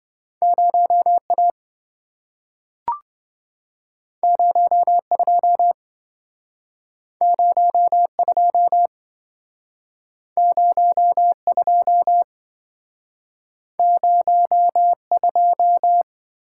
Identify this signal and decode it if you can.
Morse code, heard as a keyed tone, about 27 wpm, 700 Hz. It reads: 0A 02 02 02 02